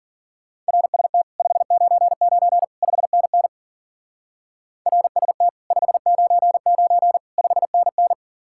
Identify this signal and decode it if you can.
Morse code, heard as a keyed tone, about 50 wpm, 700 Hz. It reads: RST 599 5NN RST 599 5NN